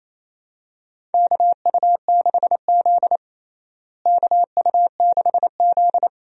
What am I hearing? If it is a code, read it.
Morse code, heard as a keyed tone, about 28 wpm, 700 Hz. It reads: KU6Z KU6Z
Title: USA Call Signs - 4 Characters - Repeated 3x - 28wpm